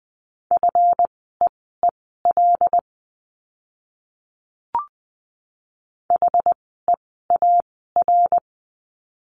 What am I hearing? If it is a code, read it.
Morse code, heard as a keyed tone, about 20 wpm, spaced as for 15 wpm, 700 Hz. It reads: FEEL HEAR
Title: 206 Lesson 33f - Letter j - Words 20wpm